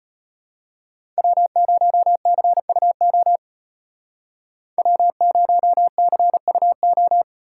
Morse code, heard as a keyed tone, about 30 wpm, 700 Hz. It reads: W0CUO W0CUO
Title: Speed Racing - USA 1x3 Call Signs 25wpm